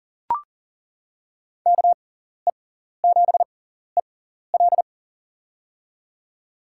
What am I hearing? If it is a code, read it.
Morse code, heard as a keyed tone, about 37 wpm, spaced as for 14 wpm, 700 Hz. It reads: KE7EL
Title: USA Call Signs - 5 Characters - 6x Character Spacing - 40wpm